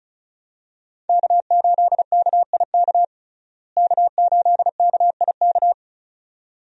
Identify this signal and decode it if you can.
Morse code, heard as a keyed tone, about 36 wpm, 700 Hz. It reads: K8KIK K8KIK
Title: USA Call Signs - 5 Characters - Repeated 2x - 35wpm